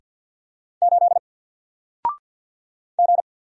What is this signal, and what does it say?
Morse code, heard as a keyed tone, about 48 wpm, 700 Hz. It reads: 8 G
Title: Single Character (Letter:Number:Punctuation:Prosign) Alt 50wpm